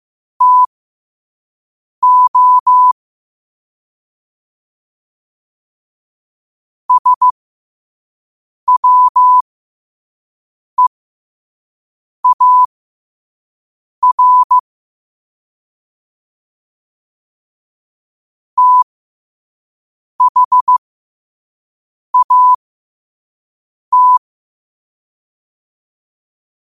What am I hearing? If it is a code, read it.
Morse code, heard as a keyed tone, about 15 wpm, spaced as for 5 wpm, 1000 Hz. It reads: TO SWEAR THAT
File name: qotdfiles / 5wpm